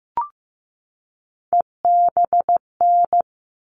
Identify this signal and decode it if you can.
Morse code, heard as a keyed tone, about 15 wpm, 700 Hz. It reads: EBN
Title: All Cut Numbers - 3-Digit 15wpm